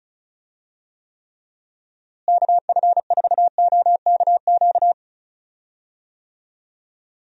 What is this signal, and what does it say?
Morse code, heard as a keyed tone, about 36 wpm, 700 Hz. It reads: KF4OKQ
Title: USA Call Signs - 6 Characters - Repeated 3x - 35wpm